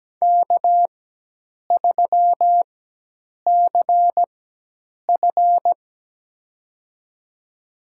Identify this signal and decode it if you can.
Morse code, heard as a keyed tone, about 17 wpm, spaced as for 8 wpm, 700 Hz. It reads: K3CF
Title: USA Call Signs - 4 Characters - 4x Character Spacing - 17wpm